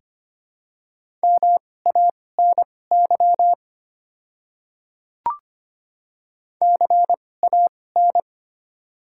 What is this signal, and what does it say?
Morse code, heard as a keyed tone, about 25 wpm, spaced as for 18 wpm, 700 Hz. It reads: MANY CAN
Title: 2x Character Spacing - Top 200 Words - 25wpm